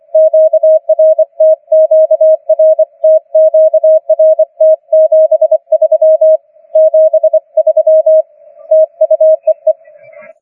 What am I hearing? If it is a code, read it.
Morse code, heard as a keyed tone, about 25 wpm, 630 Hz. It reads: QRT QRT QRT 73 73 TU EE